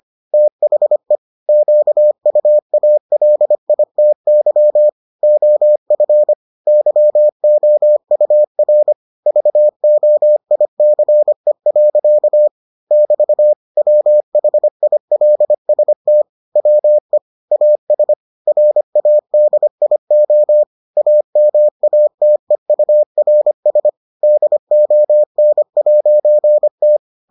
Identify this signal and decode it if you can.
Morse code, heard as a keyed tone, about 25 wpm, 600 Hz. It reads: THE QUALITY OF YOUR VOICE. <BT> WHILST WE AS RADIO AMATEURS DON'T